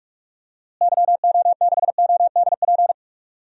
Morse code, heard as a keyed tone, about 44 wpm, 700 Hz. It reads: YO6ODP
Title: DXCC-100 Call Signs - 6 Characters - Repeated 2x - 45wpm